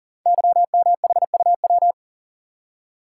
Morse code, heard as a keyed tone, about 39 wpm, 700 Hz. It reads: YMHUW
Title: Code Groups - 5 Letter:Numbers - 40wpm